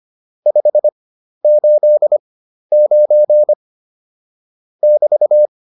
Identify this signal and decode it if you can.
Morse code, heard as a keyed tone, about 25 wpm, spaced as for 12 wpm, 600 Hz. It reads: 589 <BT>
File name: QSO 203 12